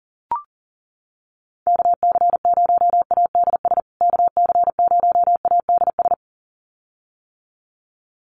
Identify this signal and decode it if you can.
Morse code, heard as a keyed tone, about 40 wpm, 700 Hz. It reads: KC0ADS KC0ADS
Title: USA Call Signs Repeated 2x 40wpm